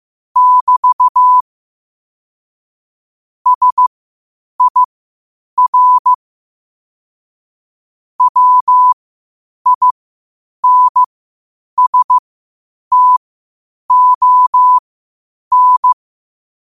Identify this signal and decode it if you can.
Morse code, heard as a keyed tone, about 15 wpm, spaced as for 8 wpm, 1000 Hz. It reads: <BT> SIR WINSTON
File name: qotdfiles / 8wpm